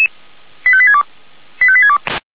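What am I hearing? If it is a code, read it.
Five-tone selective call (ZVEI), heard as quick runs of five tones, tones of about 80 ms. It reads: 85862 85862
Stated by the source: ZVEI five-tone selective call received on SDR.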